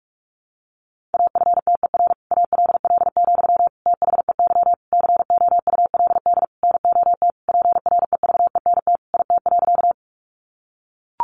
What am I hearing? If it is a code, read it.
Morse code, heard as a keyed tone, about 44 wpm, 700 Hz. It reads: AFTER ALL, THEY COULD NOT PREVENT IT.
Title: Sentences from Top 1100 Words 45wpm